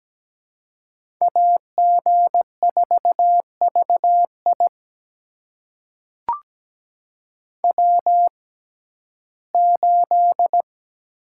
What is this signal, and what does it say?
Morse code, heard as a keyed tone, about 17 wpm, spaced as for 12 wpm, 700 Hz. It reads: AG4VI W8
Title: USA Call Signs - 5 Characters - 6x Character Spacing - 17wpm